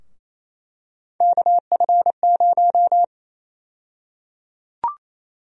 Morse code, heard as a keyed tone, about 28 wpm, 700 Hz. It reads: KF0
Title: USA Call Sign Prefixes - 28wpm